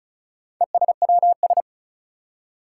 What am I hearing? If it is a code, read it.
Morse code, heard as a keyed tone, about 36 wpm, 700 Hz. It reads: ESWS
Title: Four Challenging Letter-Number Combinations 35wpm